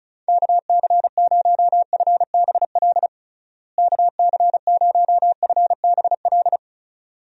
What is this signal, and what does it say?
Morse code, heard as a keyed tone, about 34 wpm, 700 Hz. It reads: KC0FBL KC0FBL
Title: USA Call Signs - 6 Characters - Repeated 2x - 35wpm